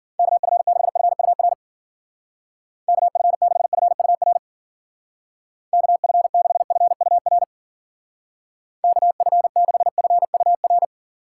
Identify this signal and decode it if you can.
Morse code, heard as a keyed tone, about 49 wpm, 700 Hz. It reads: KF6FUR KF6FUR KF6FUR KF6FUR
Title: Speed Racing - USA 2x3 Call Signs 40wpm